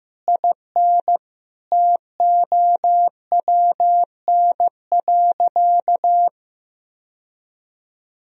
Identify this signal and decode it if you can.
Morse code, heard as a keyed tone, about 15 wpm, 700 Hz. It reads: IN TOWN.
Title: Sentences from Top 1000 Words - Review Part 4 - 15wpm